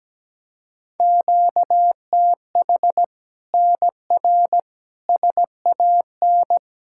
Morse code, heard as a keyed tone, about 17 wpm, 700 Hz. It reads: QTH NR SAN
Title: Practice QSO - Exchange 1 - Beginning - 17wpm